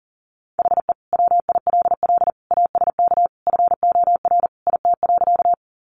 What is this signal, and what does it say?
Morse code, heard as a keyed tone, about 40 wpm, 700 Hz. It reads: HE WILL ASK FOR IT.
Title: Sentences from Top 200 Words 40wpm